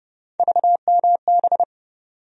Morse code, heard as a keyed tone, about 30 wpm, 700 Hz. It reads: VMB
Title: USA Call Sign Postfixes - 3 Letters 30wpm